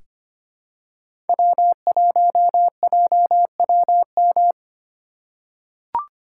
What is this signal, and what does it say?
Morse code, heard as a keyed tone, about 25 wpm, 700 Hz. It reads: W1JWM
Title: USA Call Signs - 2x Character Spacing - 25wpm